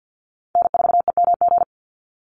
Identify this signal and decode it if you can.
Morse code, heard as a keyed tone, about 50 wpm, 700 Hz. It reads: N4EMG
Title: NA SOTA Chaser Callsigns 50wpm